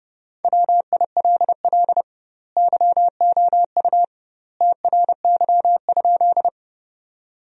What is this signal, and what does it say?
Morse code, heard as a keyed tone, about 32 wpm, 700 Hz. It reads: WILL YOU TRY?
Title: 2x Word-Spacing - Sentences from Top 200 Words - 30wpm